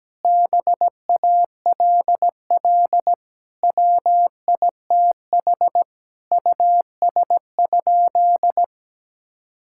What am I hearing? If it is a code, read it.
Morse code, heard as a keyed tone, about 17 wpm, 700 Hz. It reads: BALL WITH US?
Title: Sentences from Top 1000 Words - Review Part 6 - 17wpm